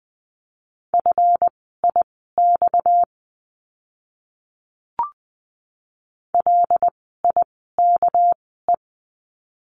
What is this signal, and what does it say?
Morse code, heard as a keyed tone, about 20 wpm, spaced as for 14 wpm, 700 Hz. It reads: FIX LIKE
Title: 230 Lesson 37f - Letter x - Words 20wpm